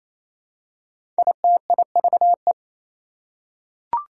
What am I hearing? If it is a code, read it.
Morse code, heard as a keyed tone, about 29 wpm, 700 Hz. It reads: ITIVE
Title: Prefixes and Suffixes 28wpm